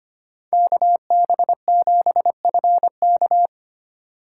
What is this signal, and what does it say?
Morse code, heard as a keyed tone, about 25 wpm, 700 Hz. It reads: KB7FK
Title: USA Call Signs - 6x Character Spacing - 25wpm